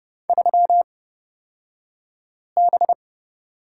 Morse code, heard as a keyed tone, about 30 wpm, spaced as for 5 wpm, 700 Hz. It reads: 3B
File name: Lesson 37d - Letter x - Two Characters 30wpm